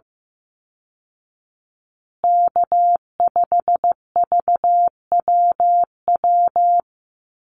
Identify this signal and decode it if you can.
Morse code, heard as a keyed tone, about 15 wpm, 700 Hz. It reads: K5VWW
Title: Call Signs Repeated 2x - 15wpm